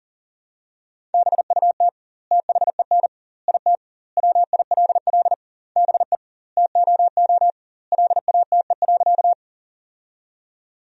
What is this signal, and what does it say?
Morse code, heard as a keyed tone, about 40 wpm, 700 Hz. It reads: BUT THEN IT WILL BE TOO LATE.
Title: 2x Word-Spacing - Sentences from Top 300 Words - 40wpm